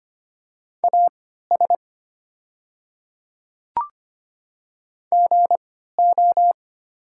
Morse code, heard as a keyed tone, about 25 wpm, spaced as for 14 wpm, 700 Hz. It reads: AS GO
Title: 3x Character Spacing - Top 100 Words - 25wpm